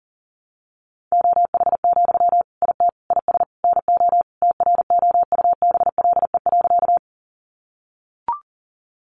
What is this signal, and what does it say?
Morse code, heard as a keyed tone, about 37 wpm, 700 Hz. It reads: OH, IT IS NO TROUBLE.
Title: Sentences from Top 1200 Words 40wpm